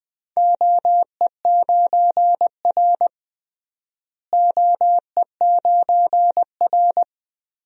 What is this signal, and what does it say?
Morse code, heard as a keyed tone, about 20 wpm, 700 Hz. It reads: OE9R OE9R
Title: DXCC-100 Call Signs - 4 Characters - Repeated 3x - 20wpm